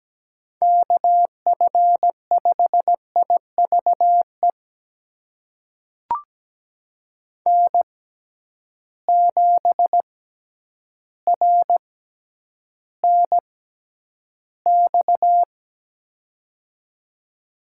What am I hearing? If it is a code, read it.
Morse code, heard as a keyed tone, about 17 wpm, spaced as for 9 wpm, 700 Hz. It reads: KF5IVE N7RNX
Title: USA Call Signs - 6x Character Spacing - 17wpm